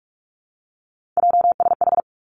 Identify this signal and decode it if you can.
Morse code, heard as a keyed tone, about 46 wpm, 700 Hz. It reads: JSH
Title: Three Challenging Letter-Number Combinations 45wpm